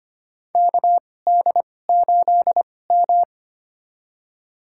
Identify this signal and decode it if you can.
Morse code, heard as a keyed tone, about 25 wpm, spaced as for 18 wpm, 700 Hz. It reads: KD8M
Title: USA Call Signs - 4 Characters - 2x Character Spacing - 25wpm